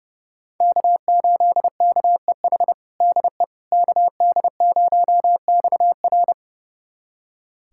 Morse code, heard as a keyed tone, about 30 wpm, 700 Hz. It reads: K8KEH DE KD0XR
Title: USA Callsign DE Callsign - 5 Characters - 30wpm